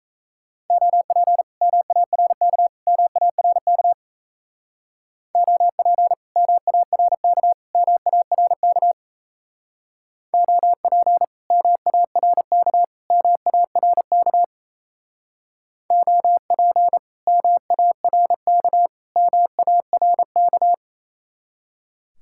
Morse code, heard as a keyed tone, about 35 wpm, 700 Hz. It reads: OP MARK MARK OP MARK MARK OP MARK MARK OP MARK MARK